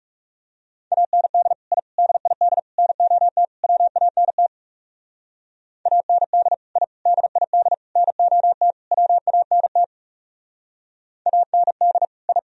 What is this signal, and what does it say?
Morse code, heard as a keyed tone, about 37 wpm, 700 Hz. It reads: AND I DID NOT WANT AND I DID NOT WANT AND I
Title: Speed-Racing - Sets of 5 Words 30wpm